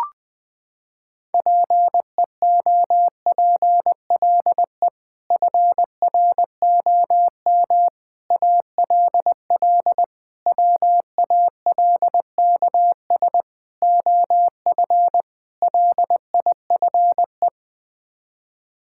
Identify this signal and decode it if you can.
Morse code, heard as a keyed tone, about 20 wpm, 700 Hz. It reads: PEOPLE FROM ALL WALKS OF LIFE
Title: Sets of 6 Words - 20wpm